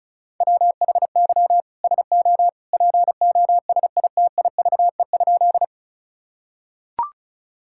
Morse code, heard as a keyed tone, about 36 wpm, 700 Hz. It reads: WHY SO POSITIVE?
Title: Sentences from Top 1000 Words - Review Part 5 - 35wpm